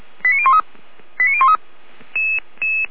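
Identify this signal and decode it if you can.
Five-tone selective call (ZVEI), heard as quick runs of five tones, tones of about 70 ms. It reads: 89013 89013
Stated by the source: ZVEI five-tone call received on SDR.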